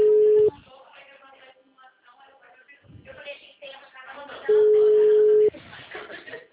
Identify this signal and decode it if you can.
Ringback tone (Europe, 425 Hz), with speech.
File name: out- -2001- - - .4